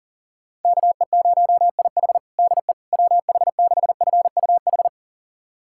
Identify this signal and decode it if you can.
Morse code, heard as a keyed tone, about 37 wpm, 700 Hz. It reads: KE0IH DE WH6FUH
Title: USA Callsign DE Callsign 40wpm